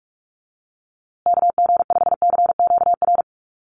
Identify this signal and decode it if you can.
Morse code, heard as a keyed tone, about 44 wpm, 700 Hz. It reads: KG5CQR
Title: USA Call Signs Repeated 3x 45wpm